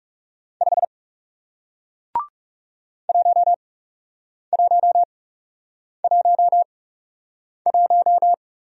Morse code, heard as a keyed tone, about 40 wpm, spaced as for 8 wpm, 700 Hz. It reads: 5 1111